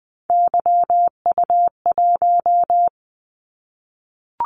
Morse code, heard as a keyed tone, about 20 wpm, 700 Hz. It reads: YU1
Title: Callsign Prefixes for Top 25 DXCC - 20wpm